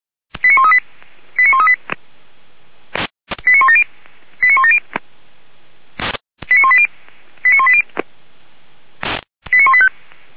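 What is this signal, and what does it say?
Five-tone selective call (ZVEI), heard as quick runs of five tones, tones of about 70 ms. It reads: 89138 89138 89170 89170 89180 89180 89185
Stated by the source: ZVEI five-tone call 89135 received on SDR, Feuerwehr Orlamünde.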